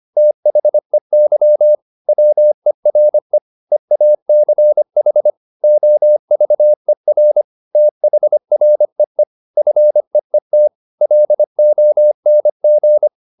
Morse code, heard as a keyed tone, about 25 wpm, 600 Hz. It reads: THEY WERE EACH OVER THREE FEET LONG